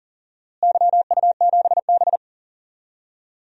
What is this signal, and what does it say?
Morse code, heard as a keyed tone, about 39 wpm, 700 Hz. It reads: YU7B
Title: DXCC-100 Call Signs - 4 Characters - Repeated 3x - 40wpm